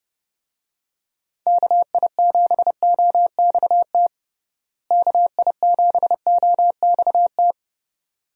Morse code, heard as a keyed tone, about 32 wpm, 700 Hz. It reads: KI7OXT KI7OXT
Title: USA Call Signs - 6 Characters - Repeated 2x - 30wpm